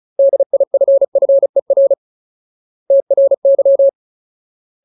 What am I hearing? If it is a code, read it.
Morse code, heard as a keyed tone, about 34 wpm, 550 Hz. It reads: DIFFER TRY
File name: Ultimate Word List / 35WPM Ultimate